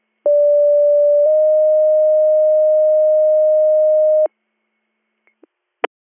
Two-tone page, heard as a one-second tone, then a three-second tone, about 580 then 620 Hz. It is heard from 0.3 to 4.3 s.